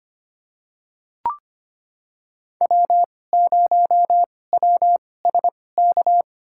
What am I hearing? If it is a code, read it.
Morse code, heard as a keyed tone, about 25 wpm, spaced as for 18 wpm, 700 Hz. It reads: W0WSK
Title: USA Call Signs - 5 Characters - 2x Character Spacing - 25wpm